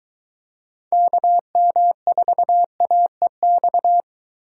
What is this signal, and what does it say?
Morse code, heard as a keyed tone, about 23 wpm, 700 Hz. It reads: KM4AEX